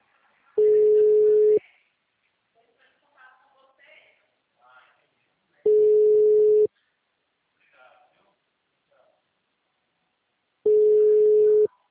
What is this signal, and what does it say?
Ringback tone (Europe, 425 Hz), with speech.